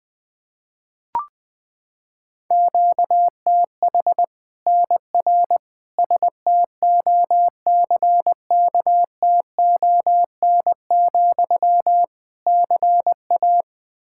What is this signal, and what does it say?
Morse code, heard as a keyed tone, about 20 wpm, 700 Hz. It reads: QTH NR STOCKTON, CA
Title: Practice QSO - Exchange 1 - Beginning - 20wpm